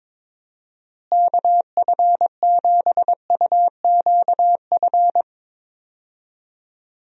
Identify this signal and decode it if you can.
Morse code, heard as a keyed tone, about 22 wpm, 700 Hz. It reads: KF7UQF
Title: USA Call Signs - 6 Characters - Repeated 2x - 22wpm